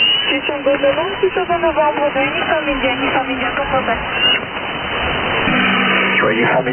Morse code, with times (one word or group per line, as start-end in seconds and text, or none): none